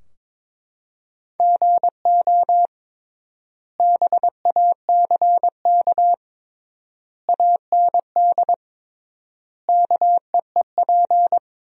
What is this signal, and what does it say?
Morse code, heard as a keyed tone, about 22 wpm, 700 Hz. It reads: GO BACK AND KEEP
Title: 3x Word-Spacing - Sentences from Top 300 Words - 22wpm